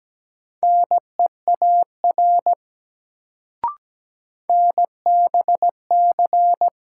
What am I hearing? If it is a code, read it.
Morse code, heard as a keyed tone, about 17 wpm, 700 Hz. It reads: NEAR NBC
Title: Sets of 5 Words Encore - New Word Drill - 17wpm